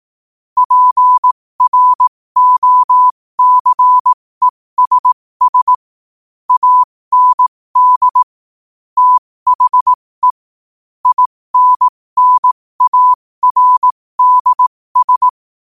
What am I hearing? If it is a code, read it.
Morse code, heard as a keyed tone, about 18 wpm, spaced as for 15 wpm, 1000 Hz. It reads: PROCESS AND THE INNARDS
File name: qotdfiles / 15wpm